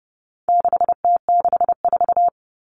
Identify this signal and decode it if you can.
Morse code, heard as a keyed tone, about 30 wpm, 700 Hz. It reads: 6T64
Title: All Cut Numbers - 4-Digit 30wpm